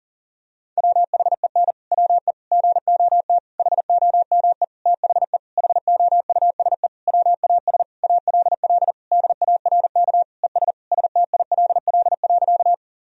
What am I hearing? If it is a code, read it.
Morse code, heard as a keyed tone, about 37 wpm, 700 Hz. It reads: WHEN WE GOT HOME THE HOUSE WAS ALL DARK ES STILL.
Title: Sentences from Top 500 Words and Abbreviations 40wpm